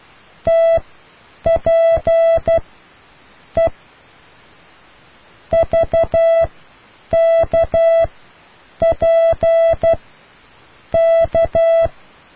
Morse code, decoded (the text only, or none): TP E VK P K